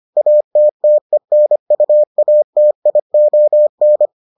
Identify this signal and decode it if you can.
Morse code, heard as a keyed tone, about 25 wpm, 600 Hz. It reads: ATTENUATION